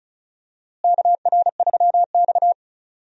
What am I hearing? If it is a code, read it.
Morse code, heard as a keyed tone, about 35 wpm, 700 Hz. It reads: KR3X
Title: USA Call Signs - 4 Characters - Repeated 2x - 35wpm